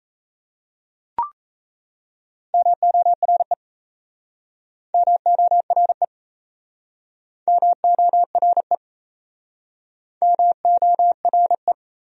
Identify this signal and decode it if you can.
Morse code, heard as a keyed tone, about 37 wpm, 700 Hz. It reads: MORE MORE MORE MORE